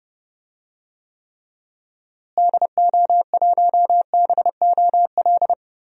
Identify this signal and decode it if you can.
Morse code, heard as a keyed tone, about 30 wpm, 700 Hz. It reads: DO1BOL
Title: DXCC-100 Call Signs - 6 Characters - Repeated 2x - 30wpm